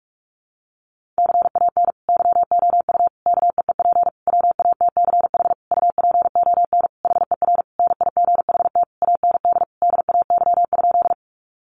Morse code, heard as a keyed tone, about 44 wpm, 700 Hz. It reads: CAN YOU KEEP WATCH UPON HER NIGHT AND DAY?
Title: Sentences from Top 1100 Words 45wpm